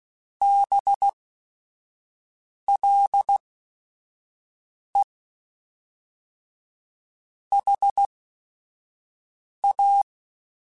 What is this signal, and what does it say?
Morse code, heard as a keyed tone, about 16 wpm, spaced as for 5 wpm, 800 Hz. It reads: BLE HA
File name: Technical Subjects / Technical Subjects-mp3s / AND Gate @5